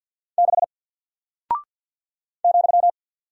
Morse code, heard as a keyed tone, about 48 wpm, 700 Hz. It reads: 6 ,
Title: Single Character (Letter:Number:Punctuation:Prosign) Alt 50wpm